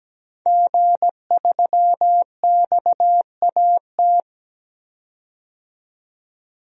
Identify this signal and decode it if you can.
Morse code, heard as a keyed tone, about 17 wpm, 700 Hz. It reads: G3XAT
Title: DXCC-100 Call Signs - 5 Characters - 17wpm